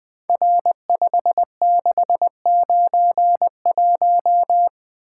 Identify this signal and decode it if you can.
Morse code, heard as a keyed tone, about 20 wpm, 700 Hz. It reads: R5691
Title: Code Groups - 5 Letter:Numbers - 20wpm